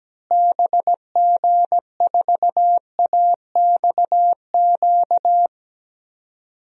Morse code, heard as a keyed tone, about 17 wpm, 700 Hz. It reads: BG4AXQ
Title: DXCC-100 Call Signs - 6 Characters - Repeated 2x - 17wpm